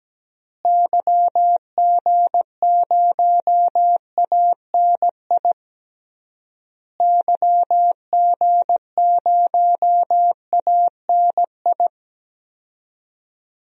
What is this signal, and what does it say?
Morse code, heard as a keyed tone, about 17 wpm, 700 Hz. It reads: YG0ANI YG0ANI
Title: DXCC-100 Call Signs - 6 Characters - Repeated 3x - 17wpm